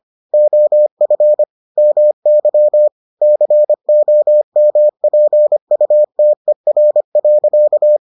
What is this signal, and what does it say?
Morse code, heard as a keyed tone, about 25 wpm, 600 Hz. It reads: OF MY COMPUTER.